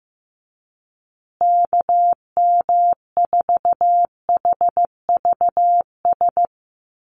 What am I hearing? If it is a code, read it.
Morse code, heard as a keyed tone, about 15 wpm, 700 Hz. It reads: KM4HVS
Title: USA Call Signs Repeated 2x 15wpm